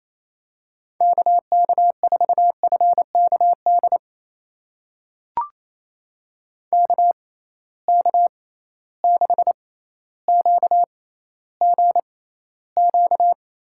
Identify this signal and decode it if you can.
Morse code, heard as a keyed tone, about 28 wpm, spaced as for 14 wpm, 700 Hz. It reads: KK4FKD KK6QGQ
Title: USA Call Signs - 6 Characters - 6x Character Spacing - 28wpm